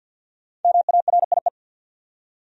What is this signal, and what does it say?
Morse code, heard as a keyed tone, about 48 wpm, 700 Hz. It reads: MARIE
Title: USA Common Names - Max 5 Letters - 50wpm